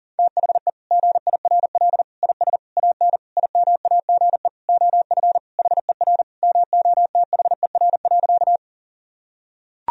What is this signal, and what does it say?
Morse code, heard as a keyed tone, about 40 wpm, 700 Hz. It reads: THE GIRL IS AN IMAGE OF HER MOTHER.
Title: Sentences from Top 1000 Words - Review Part 4 - 40wpm